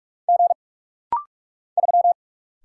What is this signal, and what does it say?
Morse code, heard as a keyed tone, about 44 wpm, 700 Hz. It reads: G 3
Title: Single Letter-Number Alt - Rapid-Fire 45wpm